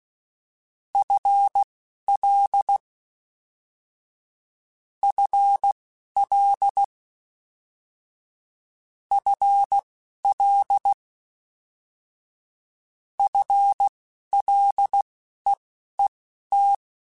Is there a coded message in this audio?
Yes: Morse code from 0.9 to 16.8 s, heard as a keyed tone, about 16 wpm, spaced as for 11 wpm, 800 Hz. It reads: FL FL FL FLEET